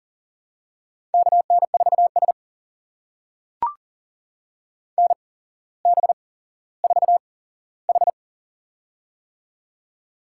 Morse code, heard as a keyed tone, about 40 wpm, spaced as for 17 wpm, 700 Hz. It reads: KN4S NB4H
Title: USA Call Signs - 4 Characters - 8x Character Spacing - 40wpm